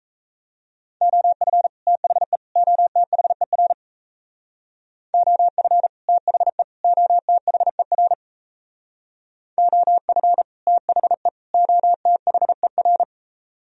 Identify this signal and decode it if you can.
Morse code, heard as a keyed tone, about 36 wpm, 700 Hz. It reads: OF THE OTHER OF THE OTHER OF THE OTHER